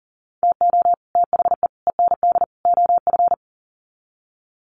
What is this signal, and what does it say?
Morse code, heard as a keyed tone, about 40 wpm, 700 Hz. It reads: TO THE END OF
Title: Sets of 4 Words 40wpm